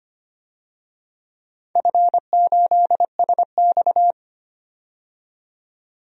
Morse code, heard as a keyed tone, about 25 wpm, 700 Hz. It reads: F8SX
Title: DXCC-100 Call Signs - 4 Characters - 25wpm